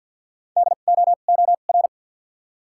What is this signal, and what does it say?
Morse code, heard as a keyed tone, about 53 wpm, spaced as for 37 wpm, 700 Hz. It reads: DOOR
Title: 2x Character Spacing - Top 500 Words - 50wpm